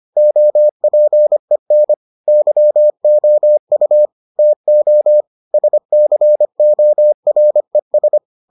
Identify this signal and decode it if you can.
Morse code, heard as a keyed tone, about 25 wpm, 600 Hz. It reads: OPEN YOU TO SCORES